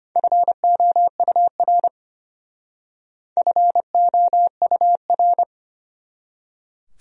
Morse code, heard as a keyed tone, about 32 wpm, 700 Hz. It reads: FOUR FOUR